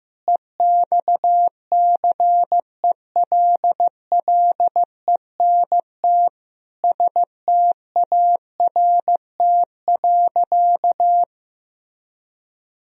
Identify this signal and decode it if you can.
Morse code, heard as a keyed tone, about 15 wpm, 700 Hz. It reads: EXCELLENT START.
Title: Sentences from Top 1000 Words - Review Part 1 - 15wpm